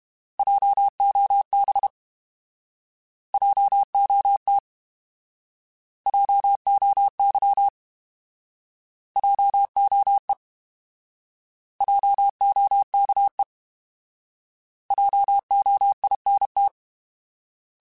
Morse code, heard as a keyed tone, about 32 wpm, 800 Hz. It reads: JOB JOT JOY JOE JOKE JOINT
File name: Alphabetical / Alphabetical mp3s / Js @30